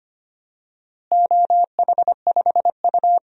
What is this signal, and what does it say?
Morse code, heard as a keyed tone, about 25 wpm, 700 Hz. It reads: OH5U
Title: DXCC-100 Call Signs - 4 Characters - Repeated 2x - 25wpm